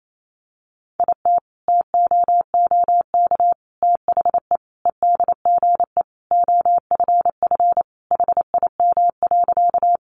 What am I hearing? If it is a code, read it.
Morse code, heard as a keyed tone, about 28 wpm, 700 Hz. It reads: IT TOOK THE EDGE OFF HIM.
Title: Sentences from Top 1300 Words 28wpm